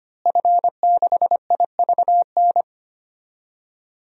Morse code, heard as a keyed tone, about 25 wpm, 700 Hz. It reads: F6IVN
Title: DXCC-100 Call Signs - 5 Characters - Repeated 3x - 25wpm